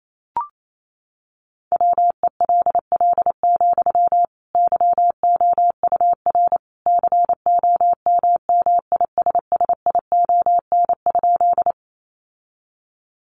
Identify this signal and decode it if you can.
Morse code, heard as a keyed tone, about 28 wpm, 700 Hz. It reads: WELL, YOUR COMMISSION?
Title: Sentences from Top 1400 Words 28wpm